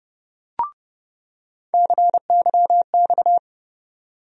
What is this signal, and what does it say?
Morse code, heard as a keyed tone, about 32 wpm, 700 Hz. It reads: CYX
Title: USA Call Sign Postfixes - 3 Letters 30wpm